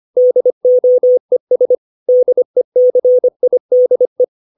Morse code, heard as a keyed tone, about 25 wpm, 500 Hz. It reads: DOES DECIDE
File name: 632 words / 632 Words 25WPM